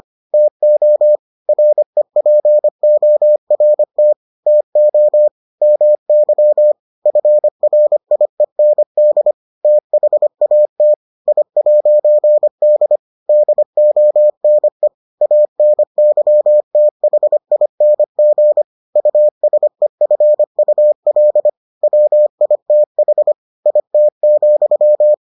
Morse code, heard as a keyed tone, about 25 wpm, 600 Hz. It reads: TO REPORT TO MY FRIEND THAT I'D DONE ANYTHING USEFUL WITH IT,